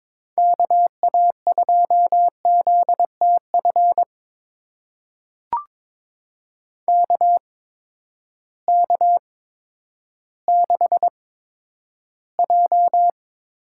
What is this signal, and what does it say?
Morse code, heard as a keyed tone, about 22 wpm, spaced as for 11 wpm, 700 Hz. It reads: KA2ZTF KK6J
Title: USA Call Signs - 6 Characters - 8x Character Spacing - 22wpm